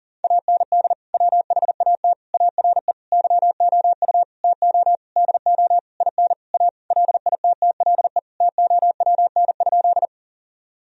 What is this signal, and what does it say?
Morse code, heard as a keyed tone, about 37 wpm, 700 Hz. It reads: AND WHAT ARE YOU TO DO IN A LITTLE TOWN?
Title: Sentences from Top 1000 Words - Review Part 2 - 40wpm